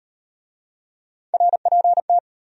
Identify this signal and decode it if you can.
Morse code, heard as a keyed tone, about 38 wpm, 700 Hz. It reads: RPT